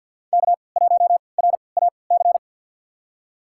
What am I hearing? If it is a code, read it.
Morse code, heard as a keyed tone, about 47 wpm, spaced as for 28 wpm, 700 Hz. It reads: K1RAC
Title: USA Call Signs - 5 Characters - 3x Character Spacing - 50wpm